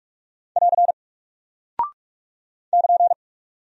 Morse code, heard as a keyed tone, about 43 wpm, 700 Hz. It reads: <AR> <KN>
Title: Single Character (Letter:Number:Punctuation:Prosign) Alt 45wpm